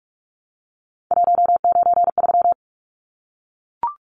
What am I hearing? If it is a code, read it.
Morse code, heard as a keyed tone, about 44 wpm, 700 Hz. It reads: 193
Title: Numbers - 3-Digit 45wpm